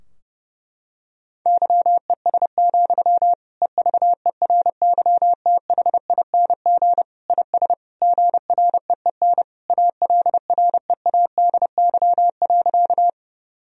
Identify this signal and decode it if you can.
Morse code, heard as a keyed tone, about 32 wpm, 700 Hz. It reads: YES, EVERYTHING IS GREEN ALREADY.
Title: Sentences from Top 1000 Words - Review Part 2 - 30wpm